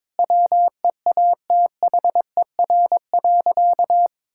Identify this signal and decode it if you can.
Morse code, heard as a keyed tone, about 21 wpm, 700 Hz. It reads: WEATHER.
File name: Sentences with 9-Word Phrases - 22wpm